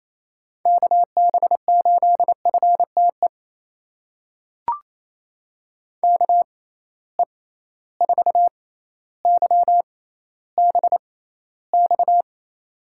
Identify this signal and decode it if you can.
Morse code, heard as a keyed tone, about 28 wpm, spaced as for 14 wpm, 700 Hz. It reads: KB8FTE KE4YBX
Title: USA Call Signs - 6 Characters - 6x Character Spacing - 28wpm